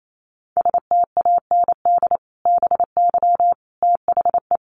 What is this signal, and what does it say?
Morse code, heard as a keyed tone, about 28 wpm, 700 Hz. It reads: STAND BY THE
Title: Sentences from Top 1200 Words 28wpm